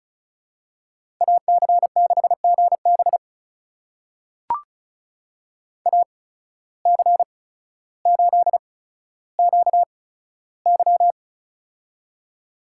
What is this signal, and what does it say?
Morse code, heard as a keyed tone, about 36 wpm, spaced as for 15 wpm, 700 Hz. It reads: AC6GB AC8QY
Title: USA Call Signs - 5 Characters - 8x Character Spacing - 35wpm